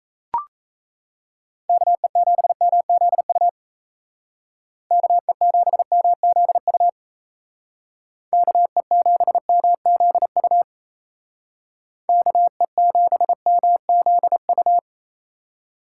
Morse code, heard as a keyed tone, about 37 wpm, 700 Hz. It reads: KE7MZU KE7MZU KE7MZU KE7MZU